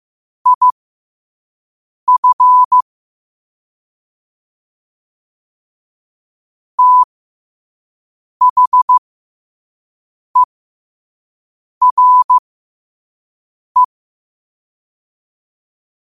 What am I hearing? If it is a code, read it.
Morse code, heard as a keyed tone, about 15 wpm, spaced as for 5 wpm, 1000 Hz. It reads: IF THERE
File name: qotdfiles / 5wpm